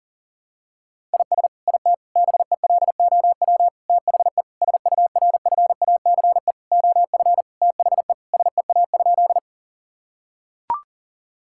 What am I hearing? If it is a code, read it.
Morse code, heard as a keyed tone, about 40 wpm, 700 Hz. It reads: IS IT BELOW THE SURFACE OF THE SEA?
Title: Sentences from Top 1000 Words - Review Part 5 - 40wpm